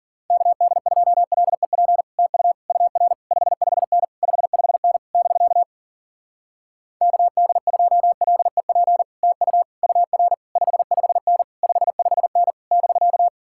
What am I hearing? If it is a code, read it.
Morse code, heard as a keyed tone, about 49 wpm, 700 Hz. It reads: KD2LEP TU UR 55N 55N <BK> KD2LEP TU UR 55N 55N <BK>
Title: Speed-Racing Common QSO Phrases Encore - 40wpm